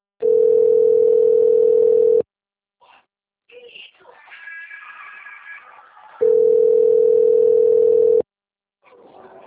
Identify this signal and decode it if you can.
Ringback tone (North America), 440 + 480 Hz, with speech.
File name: rg-1600-unknown- - - .22245